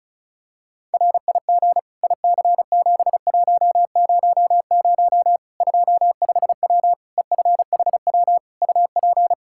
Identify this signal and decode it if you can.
Morse code, heard as a keyed tone, about 36 wpm, 700 Hz. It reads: RIG IC7100 25W EFHW UP
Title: Practice QSO - Exchange 2 - Info - 35wpm